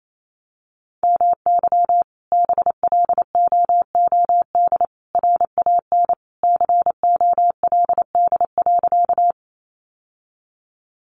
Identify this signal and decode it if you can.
Morse code, heard as a keyed tone, about 28 wpm, 700 Hz. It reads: MY BLOOD RAN COLD.
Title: Sentences from Top 1100 Words 28wpm